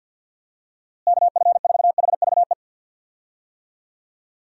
Morse code, heard as a keyed tone, about 53 wpm, 700 Hz. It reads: KF4HVE
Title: USA Call Signs - 6 Characters - Repeated 2x - 50wpm